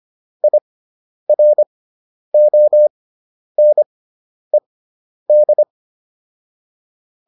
Morse code, heard as a keyed tone, about 25 wpm, spaced as for 10 wpm, 600 Hz. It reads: IRONED
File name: IR6 10